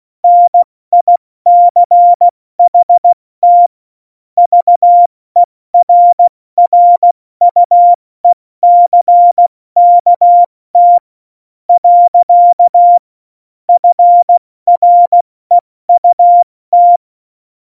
Morse code, heard as a keyed tone, about 16 wpm, 700 Hz. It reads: NICHT VERRUECKT . FREUT
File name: Morsekurs / Witze / Witz 06 March 2018, 70 F80